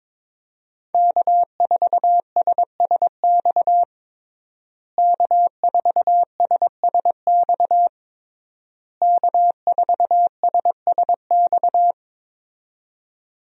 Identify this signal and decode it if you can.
Morse code, heard as a keyed tone, about 22 wpm, 700 Hz. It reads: K4SSX K4SSX K4SSX
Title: USA Call Signs - 5 Characters - Repeated 3x - 22wpm